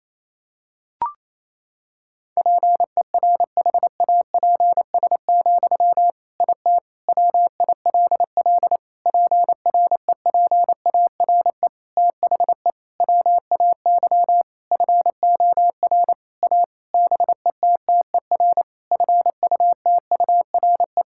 Morse code, heard as a keyed tone, about 28 wpm, 700 Hz. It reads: PERHAPS, IT WILL PREPARE THE WAY FOR A BETTER FUTURE
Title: Sentences from Top 1000 Words - Review Part 6 - 28wpm